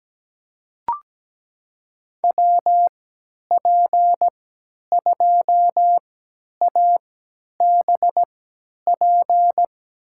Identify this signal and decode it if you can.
Morse code, heard as a keyed tone, about 17 wpm, spaced as for 10 wpm, 700 Hz. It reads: WP2ABP
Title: USA Call Signs - 3x Character Spacing - 17wpm